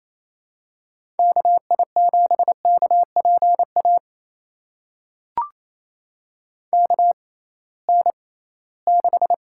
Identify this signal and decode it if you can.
Morse code, heard as a keyed tone, about 28 wpm, spaced as for 18 wpm, 700 Hz. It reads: KI7KPA KN6
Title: USA Call Signs - 6 Characters - 6x Character Spacing - 28wpm